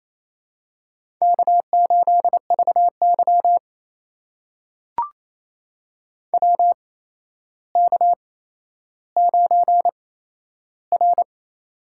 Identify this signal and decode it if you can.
Morse code, heard as a keyed tone, about 28 wpm, spaced as for 12 wpm, 700 Hz. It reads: K8VY WK9R
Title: USA Call Signs - 4 Characters - 8x Character Spacing - 28wpm